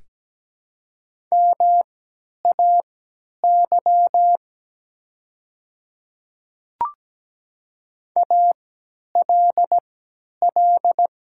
Morse code, heard as a keyed tone, about 17 wpm, spaced as for 10 wpm, 700 Hz. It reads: MAY ALL
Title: 3x Character Spacing - Top 100 Words - 17wpm